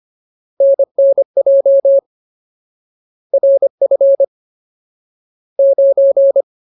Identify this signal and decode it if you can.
Morse code, heard as a keyed tone, about 25 wpm, 550 Hz. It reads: NNJ RF 9
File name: Ultimate Word List / 25WPM Ultimate